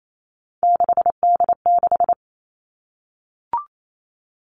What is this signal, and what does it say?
Morse code, heard as a keyed tone, about 28 wpm, 700 Hz. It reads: 6D6
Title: Three Challenging Letter-Number Combinations 28wpm